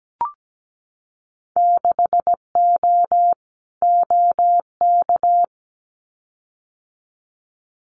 Morse code, heard as a keyed tone, about 17 wpm, 700 Hz. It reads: 6O OK
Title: Winter Field Day Exchanges Encore 17wpm